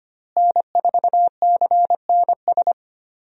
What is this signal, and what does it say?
Morse code, heard as a keyed tone, about 25 wpm, 700 Hz. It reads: N4CNS
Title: USA Call Signs - 5 Characters - Repeated 2x - 25wpm